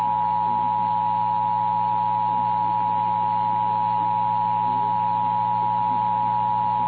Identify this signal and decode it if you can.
Emergency Alert System (SAME) tones (the 853 and 960 Hz attention tone), with noise. It is heard from 0.9 to 6.3 s.
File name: eas / KNCYKBIE / KNCYKBIE